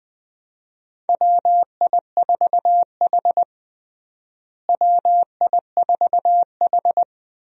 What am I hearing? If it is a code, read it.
Morse code, heard as a keyed tone, about 20 wpm, 700 Hz. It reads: WI4H WI4H
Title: USA Call Signs - 4 Characters - Repeated 2x - 20wpm